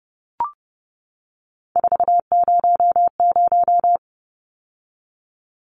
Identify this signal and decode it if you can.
Morse code, heard as a keyed tone, about 30 wpm, 700 Hz. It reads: 400
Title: Numbers - 3-Digit 30wpm